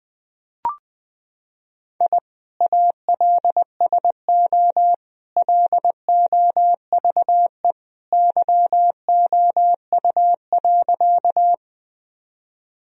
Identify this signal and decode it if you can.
Morse code, heard as a keyed tone, about 20 wpm, 700 Hz. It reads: I ALSO LOVE YOU.
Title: Sentences from Top 1000 Words - Review Part 1 - 20wpm